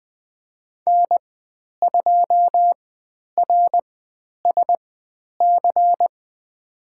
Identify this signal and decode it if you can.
Morse code, heard as a keyed tone, about 20 wpm, spaced as for 10 wpm, 700 Hz. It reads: N2RSC
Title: 109 Lesson 18g - Letter m - Call Signs 20wpm